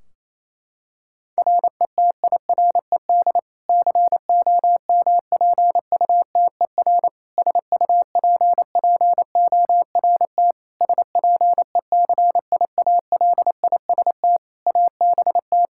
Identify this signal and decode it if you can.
Morse code, heard as a keyed tone, about 28 wpm, 700 Hz. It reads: RETIRED COMPUTER SUPPORT SPECIALIST ABT
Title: Practice QSO - Exchange 3 - More Info - 28wpm